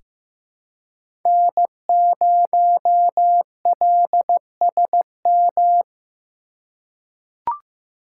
Morse code, heard as a keyed tone, about 16 wpm, 700 Hz. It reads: N0LSM
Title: USA Call Signs - 3x Character Spacing - 15wpm